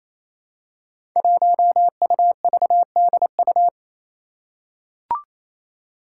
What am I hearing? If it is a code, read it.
Morse code, heard as a keyed tone, about 28 wpm, 700 Hz. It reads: 1UVDU
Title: Five Challenging Letter-Number Combinations 28wpm